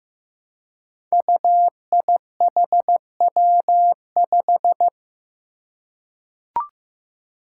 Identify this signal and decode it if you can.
Morse code, heard as a keyed tone, about 15 wpm, 700 Hz. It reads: UIHW5
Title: Five Challenging Letter-Number Combinations 15wpm